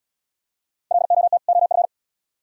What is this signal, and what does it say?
Morse code, heard as a keyed tone, about 71 wpm, 700 Hz. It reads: LOT OF